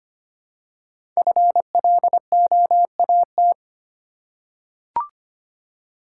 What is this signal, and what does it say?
Morse code, heard as a keyed tone, about 25 wpm, 700 Hz. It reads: FLOAT
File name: 5 Letter Words - First 3 Letters Given - 25wpm